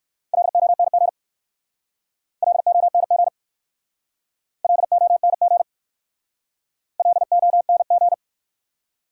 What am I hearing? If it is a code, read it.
Morse code, heard as a keyed tone, about 53 wpm, 700 Hz. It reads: LONG LONG LONG LONG
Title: Speed-Racing Top 180 QSO Elements 45wpm